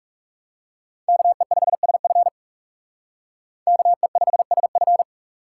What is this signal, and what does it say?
Morse code, heard as a keyed tone, about 36 wpm, 700 Hz. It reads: KE5SF KE5SF
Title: Speed Racing - USA 2x2 Call Signs 30wpm